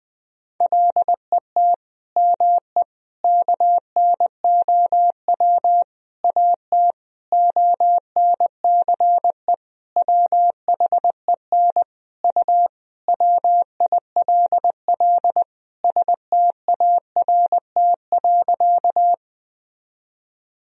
Morse code, heard as a keyed tone, about 21 wpm, 700 Hz. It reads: LET ME KNOW AT ONCE WHEN U WILL START.
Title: Sentences from Top 300 Words and Abbreviations 20wpm